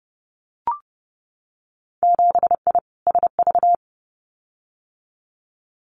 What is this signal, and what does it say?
Morse code, heard as a keyed tone, about 30 wpm, 700 Hz. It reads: 7I SV
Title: Winter Field Day Exchanges Encore 30wpm